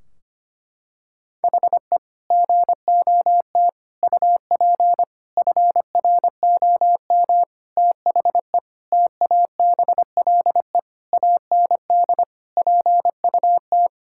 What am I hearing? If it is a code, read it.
Morse code, heard as a keyed tone, about 25 wpm, 700 Hz. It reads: HE GOT UP FROM THE TABLE AND PUT
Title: Sentences from Top 1600 Words 25wpm